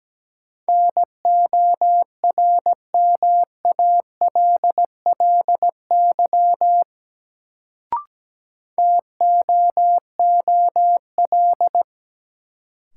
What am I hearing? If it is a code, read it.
Morse code, heard as a keyed tone, about 17 wpm, 700 Hz. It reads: NORMALLY TOOL
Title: Sentences from Top 1000 Words - Review Part 6 - New Word Drill - 17wpm